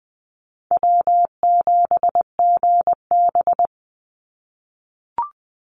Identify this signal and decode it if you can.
Morse code, heard as a keyed tone, about 20 wpm, 700 Hz. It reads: W7GB
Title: Easy Call Signs - 20wpm